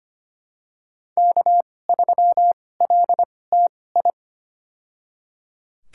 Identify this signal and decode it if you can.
Morse code, heard as a keyed tone, about 25 wpm, spaced as for 18 wpm, 700 Hz. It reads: K3LTI
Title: USA Call Signs - 2x Character Spacing - 25wpm